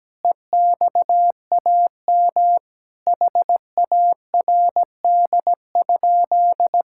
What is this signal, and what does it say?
Morse code, heard as a keyed tone, about 17 wpm, 700 Hz. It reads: EXAM HARD?
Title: Sentences from Top 1000 Words - Review Part 5 - 17wpm